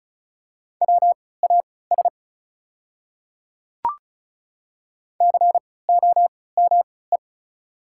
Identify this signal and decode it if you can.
Morse code, heard as a keyed tone, about 36 wpm, spaced as for 20 wpm, 690 Hz. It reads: WAS COME
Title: 3x Character Spacing - Top 100 Words - 35wpm